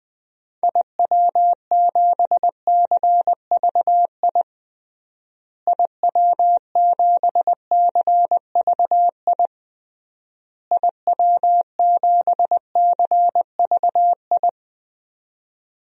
Morse code, heard as a keyed tone, about 21 wpm, 700 Hz. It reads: IW7CVI IW7CVI IW7CVI
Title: DXCC-100 Call Signs - 6 Characters - Repeated 3x - 20wpm